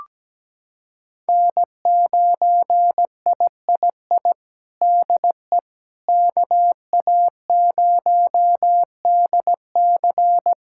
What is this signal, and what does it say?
Morse code, heard as a keyed tone, about 17 wpm, 700 Hz. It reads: N9III DE KA0DC
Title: Practice QSO - Exchange 4 - Ending - 17wpm